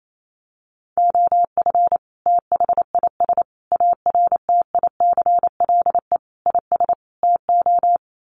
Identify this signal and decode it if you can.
Morse code, heard as a keyed tone, about 28 wpm, 700 Hz. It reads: OF THIS ARTICLE IS TO
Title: Sets of 5 Words Encore2 28wpm